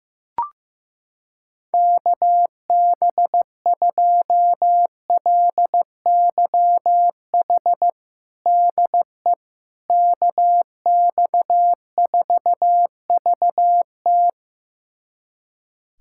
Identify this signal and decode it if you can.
Morse code, heard as a keyed tone, about 15 wpm, 700 Hz. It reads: KB2LYH DE KX4VT
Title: USA Callsign DE Callsign 15wpm